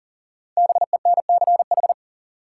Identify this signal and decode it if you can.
Morse code, heard as a keyed tone, about 37 wpm, 700 Hz. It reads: BENCH
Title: Sentences from Top 1400 Words - New Word Drill Alt - 40wpm